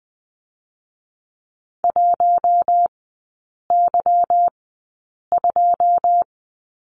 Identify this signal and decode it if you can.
Morse code, heard as a keyed tone, about 20 wpm, spaced as for 8 wpm, 690 Hz. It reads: 1Y2
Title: 145 Lesson 24e - Letter y - Three Characters 20wpm